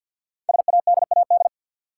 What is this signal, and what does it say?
Morse code, heard as a keyed tone, about 50 wpm, 690 Hz. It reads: SADAD
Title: Code Groups - 5 Letters - 50wpm